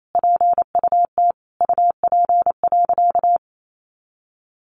Morse code, heard as a keyed tone, about 28 wpm, 700 Hz. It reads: PUT UP.
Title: A Sound Of Thunder 28wpm